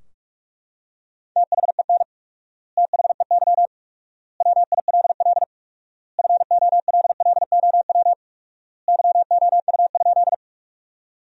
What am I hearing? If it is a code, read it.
Morse code, heard as a keyed tone, about 43 wpm, 700 Hz. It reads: THEN THEY WILL FOLLOW YOU?
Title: 4x Word-Spacing - Sentences from Top 200 Words - 45wpm